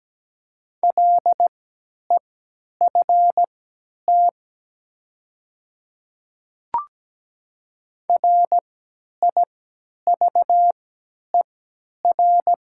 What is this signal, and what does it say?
Morse code, heard as a keyed tone, about 17 wpm, spaced as for 10 wpm, 700 Hz. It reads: LEFT RIVER
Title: 3x Character Spacing - Top 500 Words - 17wpm